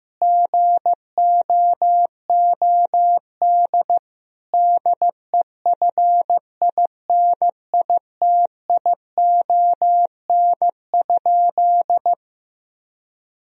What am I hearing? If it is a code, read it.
Morse code, heard as a keyed tone, about 15 wpm, 700 Hz. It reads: GOOD DEFINITION?
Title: Sentences from Top 1000 Words - Review Part 4 - 15wpm